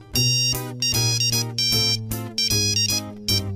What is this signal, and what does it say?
There is no code.